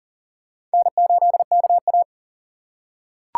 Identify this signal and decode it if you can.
Morse code, heard as a keyed tone, about 40 wpm, 700 Hz. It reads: N8KA
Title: USA Call Signs - 4 Characters - 6x Character Spacing - 40wpm